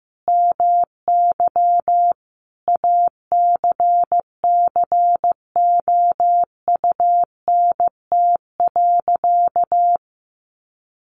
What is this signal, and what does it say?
Morse code, heard as a keyed tone, about 15 wpm, 700 Hz. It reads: MY ACCOUNT.
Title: Sentences from Top 900 Words 15wpm